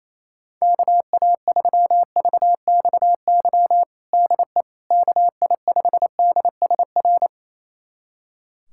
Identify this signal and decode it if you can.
Morse code, heard as a keyed tone, about 28 wpm, 700 Hz. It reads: KA3VXY DE KI5DSR
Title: USA Callsign DE Callsign 28wpm